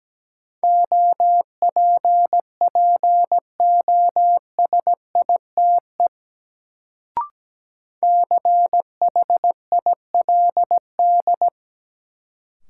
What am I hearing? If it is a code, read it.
Morse code, heard as a keyed tone, about 17 wpm, 700 Hz. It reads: OPPOSITE CHILD
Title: Sets of 5 Words Encore - New Word Drill - 17wpm